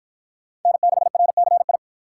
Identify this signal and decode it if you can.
Morse code, heard as a keyed tone, about 50 wpm, 700 Hz. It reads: N6RCI